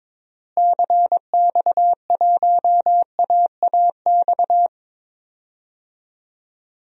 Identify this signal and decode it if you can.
Morse code, heard as a keyed tone, about 22 wpm, 700 Hz. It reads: CX1AAX
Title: DXCC-100 Call Signs - 6 Characters - Repeated 3x - 22wpm